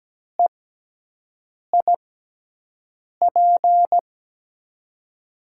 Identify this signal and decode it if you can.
Morse code, heard as a keyed tone, about 17 wpm, 700 Hz. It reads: E I P
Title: USA Call Signs - 5 Characters - 6x Character Spacing - 17wpm